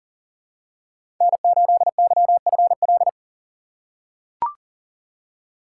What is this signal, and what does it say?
Morse code, heard as a keyed tone, about 40 wpm, 700 Hz. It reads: N8YFL
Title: USA Call Signs - 5 Characters - 4x Character Spacing - 40wpm